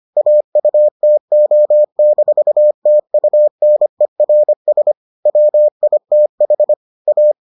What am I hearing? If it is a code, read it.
Morse code, heard as a keyed tone, about 25 wpm, 600 Hz. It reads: AUTO-TUNERS WITH A